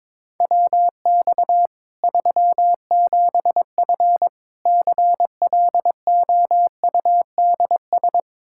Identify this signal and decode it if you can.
Morse code, heard as a keyed tone, about 22 wpm, 700 Hz. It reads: WX 37F CLOUDS
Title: Practice QSO - Exchange 2 - Info - 22wpm